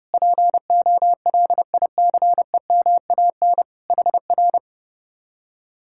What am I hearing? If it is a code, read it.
Morse code, heard as a keyed tone, about 33 wpm, 700 Hz. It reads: POLICEMAN HR